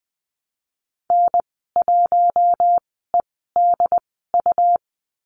Morse code, heard as a keyed tone, about 20 wpm, spaced as for 15 wpm, 700 Hz. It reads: N1EDU
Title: 147 Lesson 24g - Letter y - Call Signs 20wpm